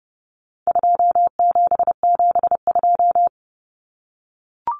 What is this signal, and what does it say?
Morse code, heard as a keyed tone, about 30 wpm, 700 Hz. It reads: 2772
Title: Common Cut Numbers - 4-Digit 30wpm